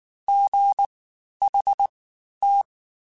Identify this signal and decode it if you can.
Morse code, heard as a keyed tone, about 19 wpm, spaced as for 11 wpm, 800 Hz. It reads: GHT
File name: Special INGs-mp3s / Special INGs-mp3s / Fast ing @18